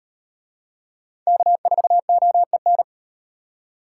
Morse code, heard as a keyed tone, about 38 wpm, 700 Hz. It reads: K4OEN